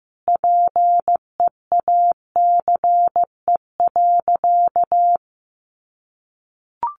Morse code, heard as a keyed tone, about 15 wpm, 700 Hz. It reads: PEACE.
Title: Sentences from Top 1500 Words 15wpm